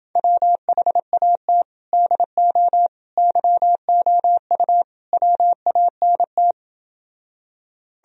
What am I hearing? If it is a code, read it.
Morse code, heard as a keyed tone, about 28 wpm, 700 Hz. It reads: WHAT DO YOU WANT